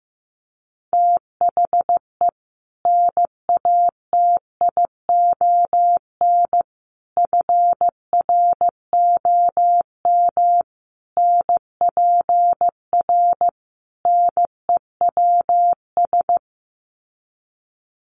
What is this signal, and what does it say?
Morse code, heard as a keyed tone, about 15 wpm, 700 Hz. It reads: THE NATION FROM NPR NEWS
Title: Sets of 5 Words 15wpm